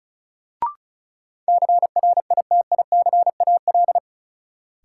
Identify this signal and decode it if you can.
Morse code, heard as a keyed tone, about 36 wpm, 700 Hz. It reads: CRITICAL
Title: Sentences from Top 1100 Words - New Word Drill - 35wpm